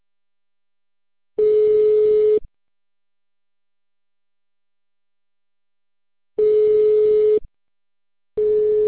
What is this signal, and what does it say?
Ringback tone (Europe, 425 Hz).